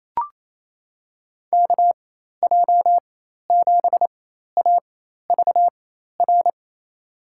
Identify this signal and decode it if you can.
Morse code, heard as a keyed tone, about 28 wpm, spaced as for 13 wpm, 700 Hz. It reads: KJ7AVR
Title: USA Call Signs - 6 Characters - 4x Character Spacing - 28wpm